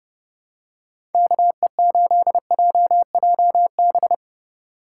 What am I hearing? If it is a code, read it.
Morse code, heard as a keyed tone, about 30 wpm, 700 Hz. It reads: KE8JJB
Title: USA Call Signs - 6 Characters - 6x Character Spacing - 30wpm